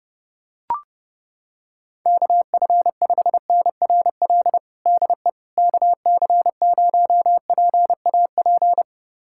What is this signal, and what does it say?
Morse code, heard as a keyed tone, about 32 wpm, 700 Hz. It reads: KF5NRL DE KC0PAP
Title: USA Callsign DE Callsign 30wpm